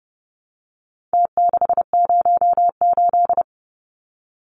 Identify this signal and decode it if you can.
Morse code, heard as a keyed tone, about 30 wpm, 700 Hz. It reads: T608
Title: Numbers - Contest Style - 4-Digit 30wpm